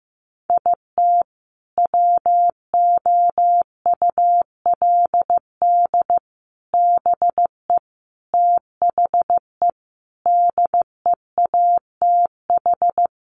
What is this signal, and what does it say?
Morse code, heard as a keyed tone, about 15 wpm, 700 Hz. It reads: IT WOULD BE THE DEATH
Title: Sentences from Top 1400 Words 15wpm